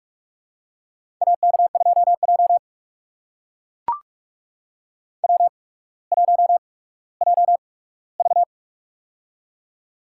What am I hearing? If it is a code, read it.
Morse code, heard as a keyed tone, about 45 wpm, spaced as for 19 wpm, 700 Hz. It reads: AK2J W1JV
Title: USA Call Signs - 4 Characters - 8x Character Spacing - 45wpm